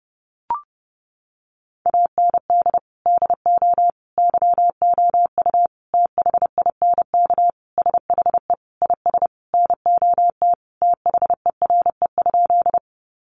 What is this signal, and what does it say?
Morse code, heard as a keyed tone, about 30 wpm, 700 Hz. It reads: AND DO YOU THINK SHE IS NOT THERE?
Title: Sentences from Top 600 Words 30wpm